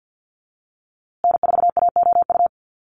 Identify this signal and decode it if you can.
Morse code, heard as a keyed tone, about 50 wpm, 700 Hz. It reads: N4AOU
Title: USA Call Signs Repeated 2x 50wpm